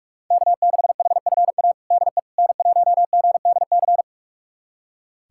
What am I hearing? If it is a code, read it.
Morse code, heard as a keyed tone, about 46 wpm, 700 Hz. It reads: K6HFA DE N1GDC
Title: USA Callsign DE Callsign - 5 Characters - 45wpm